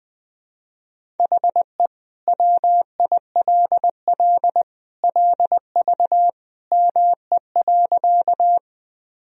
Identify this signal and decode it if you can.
Morse code, heard as a keyed tone, about 20 wpm, 700 Hz. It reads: HE WILL LV ME.
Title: Sentences from Top 400 Words and Abbreviations 20wpm